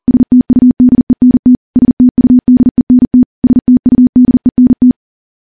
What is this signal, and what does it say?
Morse code, heard as a keyed tone, about 43 wpm, 250 Hz. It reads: STUDENT STUDENT STUDENT
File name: morse / morse-training-40WPM